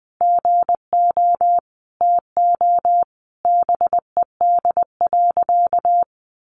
Morse code, heard as a keyed tone, about 20 wpm, 700 Hz. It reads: GO TO BED.
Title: Sentences from Top 700 Words 20wpm